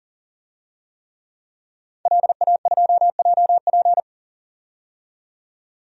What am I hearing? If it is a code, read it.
Morse code, heard as a keyed tone, about 40 wpm, 700 Hz. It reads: LA2JP
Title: DXCC-100 Call Signs - 5 Characters - 40wpm